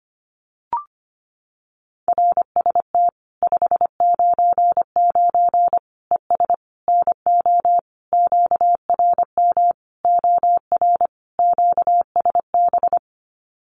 Morse code, heard as a keyed tone, about 25 wpm, 700 Hz. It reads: RST 599 ES NO QRM OR QSB
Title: Common QSO Phrases 25wpm